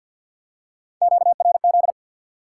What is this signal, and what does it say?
Morse code, heard as a keyed tone, about 53 wpm, 700 Hz. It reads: QRZ
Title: Top 90 QSO Elements 50wpm